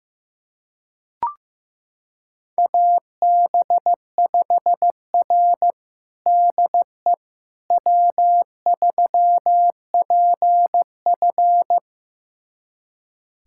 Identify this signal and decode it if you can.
Morse code, heard as a keyed tone, about 15 wpm, 700 Hz. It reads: AB5R DE W3PF
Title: USA Callsign DE Callsign - 4 Characters - 15wpm